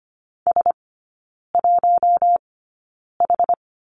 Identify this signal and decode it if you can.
Morse code, heard as a keyed tone, about 25 wpm, spaced as for 9 wpm, 700 Hz. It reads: S1H
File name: Lesson 17e - Letter u - Three Characters 25wpm